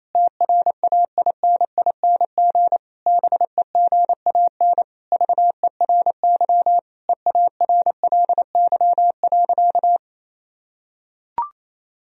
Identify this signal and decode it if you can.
Morse code, heard as a keyed tone, about 28 wpm, 700 Hz. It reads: TRAINING BEGAN VERY EARLY.
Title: Sentences from Top 1000 Words - Review Part 4 - 28wpm